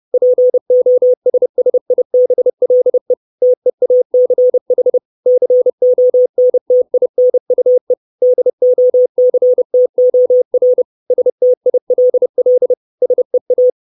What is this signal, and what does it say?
Morse code, heard as a keyed tone, about 30 wpm, 500 Hz. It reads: POSSIBLE TEACH CONTINUE DOCTOR STILL SEA
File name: 632 Words 30WPM